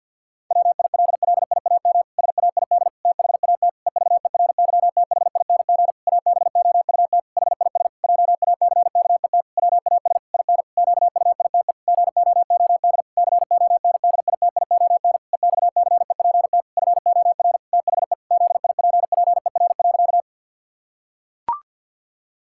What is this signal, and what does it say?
Morse code, heard as a keyed tone, about 50 wpm, 700 Hz. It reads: WILLIAM SAID THAT EVERYTHING ABOUT HIS JACKET WAS IN QUITE GOOD CONDITION EXCEPT FOR THE ZIPPER.
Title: Pangram Challenge v2 - 50wpm